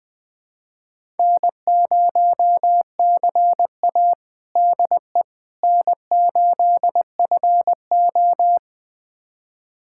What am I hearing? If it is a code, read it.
Morse code, heard as a keyed tone, about 21 wpm, 700 Hz. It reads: N0CA DE N8FO
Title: USA Callsign DE Callsign - 4 Characters - 20wpm